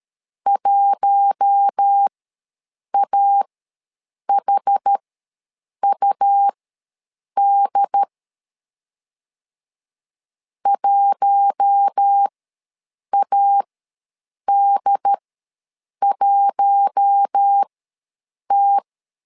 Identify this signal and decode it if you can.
Morse code, heard as a keyed tone, about 13 wpm, spaced as for 7 wpm, 800 Hz. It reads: 1AHUD 1AD1T
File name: LEZIONI / 4 Q1 ws 12wpm 2129